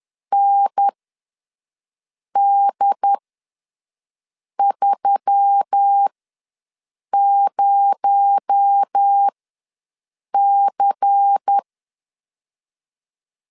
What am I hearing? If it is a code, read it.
Morse code, heard as a keyed tone, about 10 wpm, spaced as for 5 wpm, 800 Hz. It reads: ND30C
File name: LEZIONI / 7 GF9Z / 7 GF9Z ws 10wpm 3919